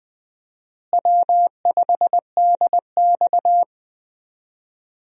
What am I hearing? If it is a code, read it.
Morse code, heard as a keyed tone, about 21 wpm, 700 Hz. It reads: W5DX
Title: USA Call Signs - 4 Characters - Repeated 2x - 20wpm